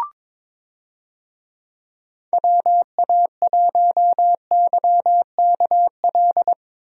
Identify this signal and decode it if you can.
Morse code, heard as a keyed tone, about 22 wpm, 700 Hz. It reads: WA1YKL
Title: USA Call Signs - 6 Characters - Repeated 2x - 22wpm